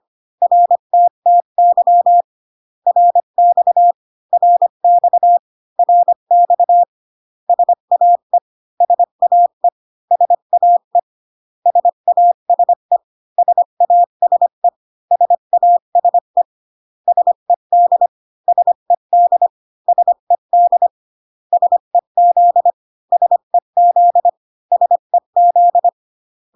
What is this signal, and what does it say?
Morse code, heard as a keyed tone, about 25 wpm, 700 Hz. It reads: RTTY RX RX RX SAE SAE SAE SASE SASE SASE SED SED SED SEZ SEZ SEZ